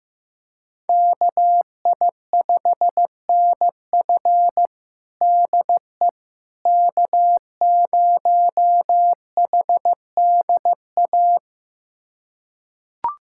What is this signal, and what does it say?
Morse code, heard as a keyed tone, about 16 wpm, 700 Hz. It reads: KI5NF DE K0HDA
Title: USA Callsign DE Callsign - 5 Characters - 15wpm